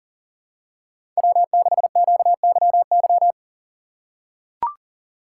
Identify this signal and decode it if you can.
Morse code, heard as a keyed tone, about 40 wpm, 700 Hz. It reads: W6QYY
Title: USA Call Signs - 3x Character Spacing - 40wpm